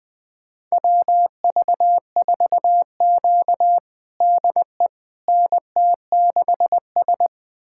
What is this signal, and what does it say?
Morse code, heard as a keyed tone, about 21 wpm, 700 Hz. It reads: WV4Q DE NT6S
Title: USA Callsign DE Callsign - 4 Characters - 20wpm